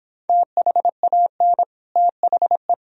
Morse code, heard as a keyed tone, about 26 wpm, 700 Hz. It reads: THAN THE